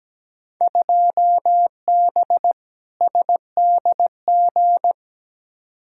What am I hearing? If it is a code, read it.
Morse code, heard as a keyed tone, about 17 wpm, 700 Hz. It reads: 2B SDG
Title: Field Day Exchanges Encore - 17wpm